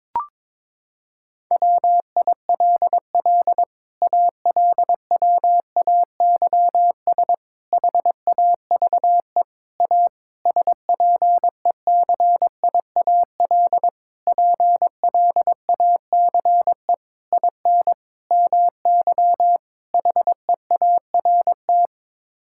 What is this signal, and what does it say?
Morse code, heard as a keyed tone, about 22 wpm, 700 Hz. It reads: WILL ALWAYS HAVE A SPECIAL PLACE IN MY HEART
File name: Sets of 9 Words - 22wpm